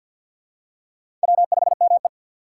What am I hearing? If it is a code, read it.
Morse code, heard as a keyed tone, about 50 wpm, 700 Hz. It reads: W5ME
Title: USA Call Signs - 4 Characters - 6x Character Spacing - 50wpm